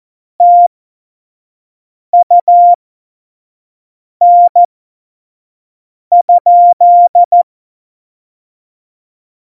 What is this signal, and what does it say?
Morse code, heard as a keyed tone, about 14 wpm, spaced as for 5 wpm, 700 Hz. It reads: TUN?
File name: Morsekurs / Witze / Witz 09. Januar 2018, 25 F70